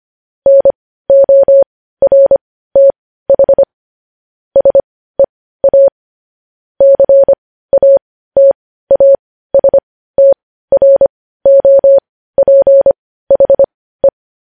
Morse code, heard as a keyed tone, about 25 wpm, spaced as for 15 wpm, 550 Hz. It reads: NORTH SEA CATASTROPHE
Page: News headlines in Morse code at 15 wpm